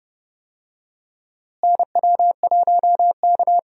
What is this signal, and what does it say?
Morse code, heard as a keyed tone, about 30 wpm, 700 Hz. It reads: NW1K
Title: USA Call Signs - 4 Characters - Repeated 3x - 30wpm